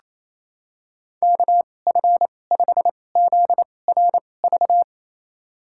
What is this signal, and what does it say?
Morse code, heard as a keyed tone, about 28 wpm, spaced as for 20 wpm, 690 Hz. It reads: KF5ZRV
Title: USA Call Signs - 6 Characters - 2x Character Spacing - 28wpm